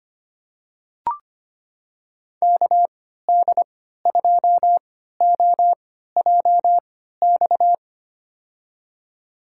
Morse code, heard as a keyed tone, about 25 wpm, spaced as for 14 wpm, 700 Hz. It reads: KD2OJX
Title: USA Call Signs - 6 Characters - 3x Character Spacing - 25wpm